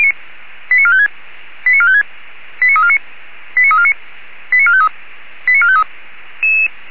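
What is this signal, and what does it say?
Five-tone selective call (ZVEI), heard as quick runs of five tones, tones of about 80 ms. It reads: 89456 89456 89349 89349 89453 89453
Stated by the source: ZVEI five-tone selective call code 89350, received by SDR.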